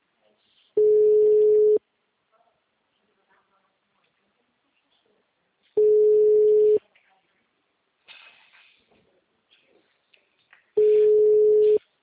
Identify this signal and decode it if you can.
Ringback tone (Europe, 425 Hz), with speech.
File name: out- -2001- - - .57720